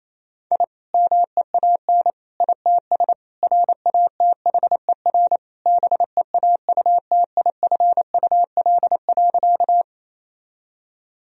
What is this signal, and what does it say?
Morse code, heard as a keyed tone, about 28 wpm, 700 Hz. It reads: I MEAN ITS RATHER BEAUTIFUL.
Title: Sentences from Top 1000 Words - Review Part 4 - 28wpm